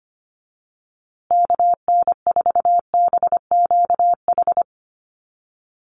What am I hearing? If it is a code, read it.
Morse code, heard as a keyed tone, about 25 wpm, 700 Hz. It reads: KN4BQH
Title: USA Call Signs Repeated 2x 25wpm